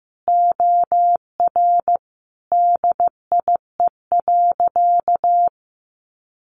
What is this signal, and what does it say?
Morse code, heard as a keyed tone, about 15 wpm, 700 Hz. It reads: OR DIE.
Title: Sentences from Top 1300 Words 15wpm